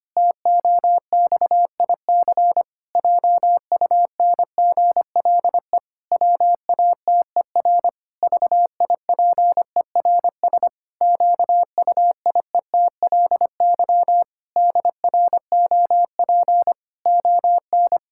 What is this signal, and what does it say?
Morse code, heard as a keyed tone, about 25 wpm, 700 Hz. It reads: TOXIC JUNGLE WATER VIPERS QUIETLY DROP ON
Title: Pangram Challenge v2 - 25wpm